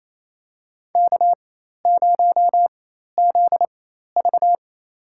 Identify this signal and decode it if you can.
Morse code, heard as a keyed tone, about 28 wpm, spaced as for 13 wpm, 700 Hz. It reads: K0ZV
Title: USA Call Signs - 4 Characters - 4x Character Spacing - 28wpm